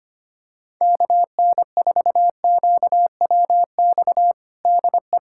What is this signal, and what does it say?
Morse code, heard as a keyed tone, about 25 wpm, 700 Hz. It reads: KN4QWX DE
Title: USA Callsign DE Callsign - 6 Characters - 25wpm